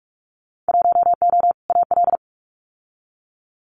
Morse code, heard as a keyed tone, about 46 wpm, 700 Hz. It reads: 1O AL
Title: Winter Field Day Exchanges 45wpm